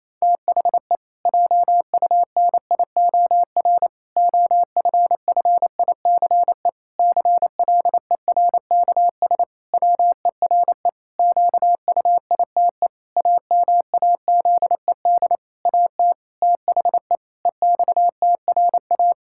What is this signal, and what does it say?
Morse code, heard as a keyed tone, about 28 wpm, 700 Hz. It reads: THE JUNIOR OFFICE CLERKS WERE QUITE AMAZED AT THE EXTRA
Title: Pangram Challenge v2 - 28wpm